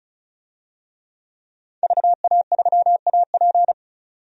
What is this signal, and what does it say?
Morse code, heard as a keyed tone, about 34 wpm, 700 Hz. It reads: VA3AP
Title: DXCC-100 Call Signs - 5 Characters - 35wpm